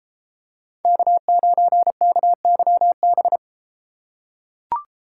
Morse code, heard as a keyed tone, about 30 wpm, 700 Hz. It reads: K9KYB